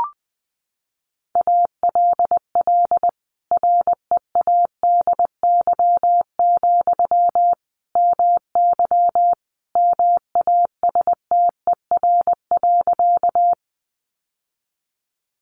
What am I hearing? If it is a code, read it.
Morse code, heard as a keyed tone, about 20 wpm, 700 Hz. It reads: ALL READY, MY MASTER.
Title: Sentences from Top 1300 Words 20wpm